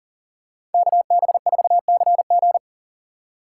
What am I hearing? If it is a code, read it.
Morse code, heard as a keyed tone, about 40 wpm, 700 Hz. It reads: KB4CG
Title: K1USN SST QSO Practice - 40wpm